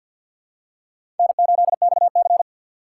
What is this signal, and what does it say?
Morse code, heard as a keyed tone, about 51 wpm, 700 Hz. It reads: N8XC
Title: USA Call Signs - 4 Characters - 2x Character Spacing - 50wpm